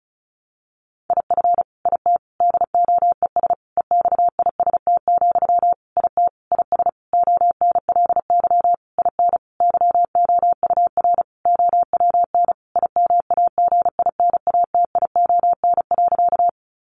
Morse code, heard as a keyed tone, about 34 wpm, 700 Hz. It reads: IF IT DOES EXIST, IT IS ONLY IN YOUR OWN IMAGINATION.
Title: Sentences from Top 1300 Words 35wpm